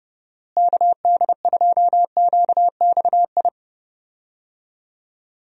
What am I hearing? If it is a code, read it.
Morse code, heard as a keyed tone, about 30 wpm, 700 Hz. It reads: KD2QXI
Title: USA Call Signs - 6 Characters - Repeated 3x - 30wpm